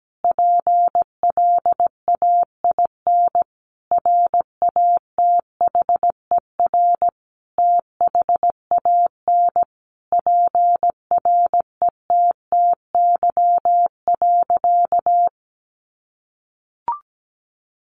Morse code, heard as a keyed tone, about 17 wpm, 700 Hz. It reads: PLAIN RATHER THAN PRETTY.
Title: Sentences from Top 1000 Words 17wpm